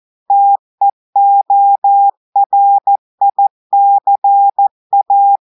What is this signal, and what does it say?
Morse code, heard as a keyed tone, about 14 wpm, 800 Hz. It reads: TEORICA
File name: audios / portugais / portugais texte en clair 1 vitesse 14 mots minutes